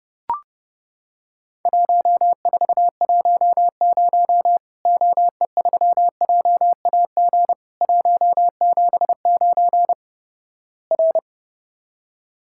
Morse code, heard as a keyed tone, about 29 wpm, 700 Hz. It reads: 1410 OE3JAG 179 R
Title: Worked All Europe DX Contest - QTC Exchange - 30wpm